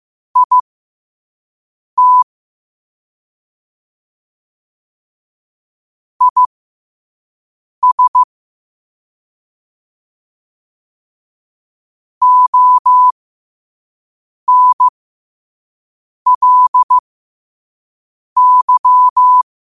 Morse code, heard as a keyed tone, about 15 wpm, spaced as for 5 wpm, 1000 Hz. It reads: IT IS ONLY